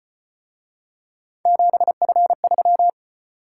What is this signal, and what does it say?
Morse code, heard as a keyed tone, about 35 wpm, 700 Hz. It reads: 7F3